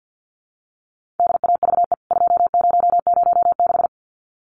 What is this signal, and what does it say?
Morse code, heard as a keyed tone, about 50 wpm, 700 Hz. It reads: DAVE 2006
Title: CWT Exchange 50wpm